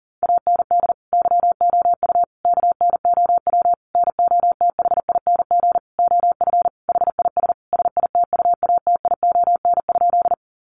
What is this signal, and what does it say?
Morse code, heard as a keyed tone, about 40 wpm, 700 Hz. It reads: AND YOU KNOW NOTHING OF HIS SITUATION?
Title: Sentences from Top 1200 Words 40wpm